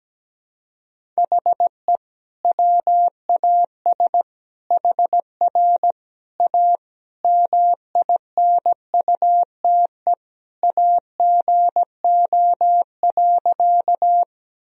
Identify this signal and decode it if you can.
Morse code, heard as a keyed tone, about 17 wpm, 700 Hz. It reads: HE WAS HR A MINUTE AGO.
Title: Sentences from Top 500 Words and Abbreviations 17wpm